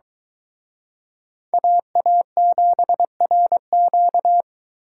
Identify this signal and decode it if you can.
Morse code, heard as a keyed tone, about 22 wpm, 700 Hz. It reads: AA7RQ